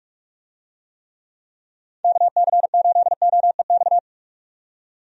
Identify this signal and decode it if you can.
Morse code, heard as a keyed tone, about 42 wpm, 700 Hz. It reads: KC8OEX
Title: USA Call Signs - 6 Characters - 45wpm